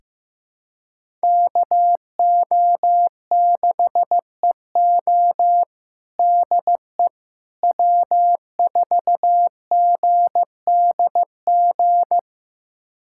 Morse code, heard as a keyed tone, about 15 wpm, 700 Hz. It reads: KO6EO DE W4GDG
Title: USA Callsign DE Callsign - 5 Characters - 15wpm